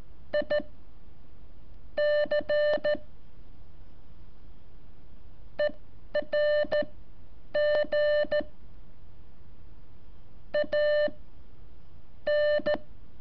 Morse code, heard as a keyed tone, about 14 wpm, 600 Hz. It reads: I C E R G A N
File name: curso / ejercicio 20